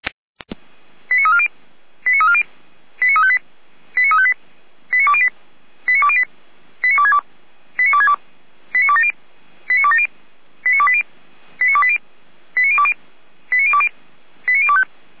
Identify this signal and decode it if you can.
Five-tone selective call (ZVEI), heard as quick runs of five tones, tones of about 70 ms. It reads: 89340 89340 89358 89358 89208 89208 89262 89262 89270 89270 89280 89280 89020 89020 89025